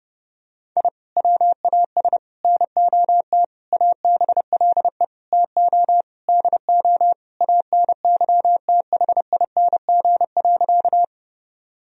Morse code, heard as a keyed tone, about 30 wpm, 700 Hz. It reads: I WAS NOT ABLE TO DO ANYTHING.
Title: Sentences from Top 1000 Words - Review Part 2 - 30wpm